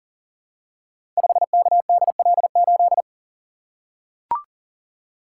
Morse code, heard as a keyed tone, about 39 wpm, 700 Hz. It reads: 5KDL8
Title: Code Groups - 5 Letter:Numbers - 40wpm